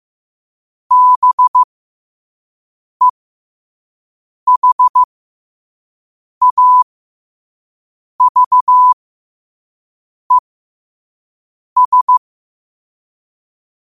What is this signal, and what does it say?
Morse code, heard as a keyed tone, about 15 wpm, spaced as for 5 wpm, 1000 Hz. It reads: BEHAVES
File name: 5wpm